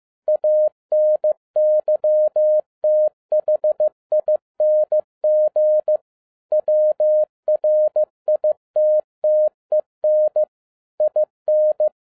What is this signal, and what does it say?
Morse code, heard as a keyed tone, about 15 wpm, 610 Hz. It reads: ANYTHING WRITTEN IN